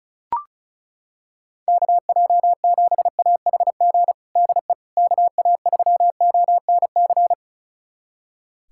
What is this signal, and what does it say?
Morse code, heard as a keyed tone, about 34 wpm, 700 Hz. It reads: KJ7AHG DE KA3ONC
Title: USA Callsign DE Callsign 35wpm